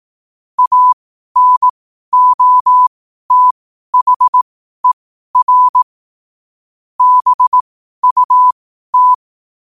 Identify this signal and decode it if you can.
Morse code, heard as a keyed tone, about 18 wpm, spaced as for 12 wpm, 1000 Hz. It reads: ANOTHER BUT
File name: qotdfiles / 12wpm